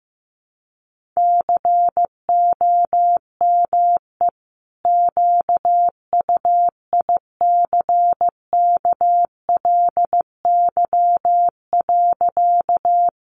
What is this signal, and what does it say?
Morse code, heard as a keyed tone, about 15 wpm, 700 Hz. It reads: COME QUICKLY.
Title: Sentences from Top 700 Words 15wpm